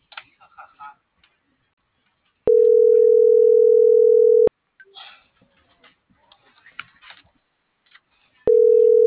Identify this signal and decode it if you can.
Ringback tone (North America), 440 + 480 Hz.